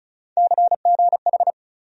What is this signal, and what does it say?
Morse code, heard as a keyed tone, about 34 wpm, 700 Hz. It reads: CGH
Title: USA Call Sign Postfixes - 3 Letters 35wpm